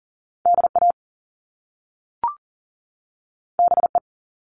Morse code, heard as a keyed tone, about 40 wpm, 700 Hz. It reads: DA BE
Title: Top Two Letter Combinations in Words 40wpm